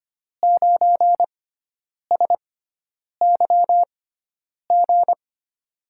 Morse code, heard as a keyed tone, about 25 wpm, spaced as for 9 wpm, 700 Hz. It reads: 9SYG
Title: USA Call Signs - 6 Characters - 6x Character Spacing - 25wpm